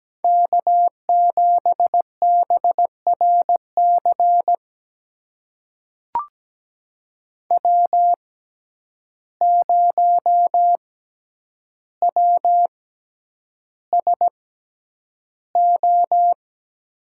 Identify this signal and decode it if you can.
Morse code, heard as a keyed tone, about 17 wpm, spaced as for 9 wpm, 700 Hz. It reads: K7BRC W0WSO
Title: USA Call Signs - 5 Characters - 6x Character Spacing - 17wpm